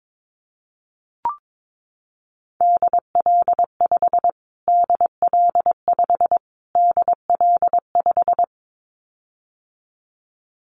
Morse code, heard as a keyed tone, about 22 wpm, 700 Hz. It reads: DL5 DL5 DL5
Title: Callsign Prefixes for Top 25 DXCC - Repeated 3x - 22wpm